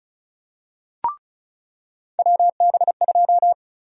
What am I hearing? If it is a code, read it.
Morse code, heard as a keyed tone, about 36 wpm, 700 Hz. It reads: WB2
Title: USA Call Sign Prefixes - 35wpm